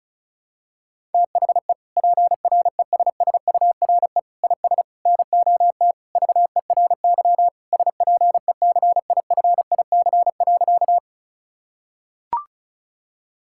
Morse code, heard as a keyed tone, about 34 wpm, 700 Hz. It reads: THE PRESSURE IS NOT VERY SPECIFIC.
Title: Sentences from Top 1000 Words - Review Part 6 - 35wpm